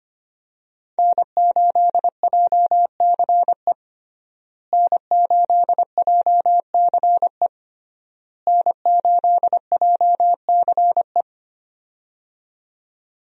Morse code, heard as a keyed tone, about 25 wpm, 700 Hz. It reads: N8JCE N8JCE N8JCE
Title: USA Call Signs - 5 Characters - Repeated 3x - 25wpm